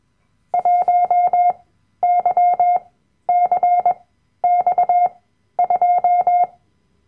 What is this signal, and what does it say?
Morse code, heard as a keyed tone, about 21 wpm, spaced as for 12 wpm, 710 Hz. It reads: 1YCX2
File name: Workshop Dateien / CW-Online-Kurs / Lektion 010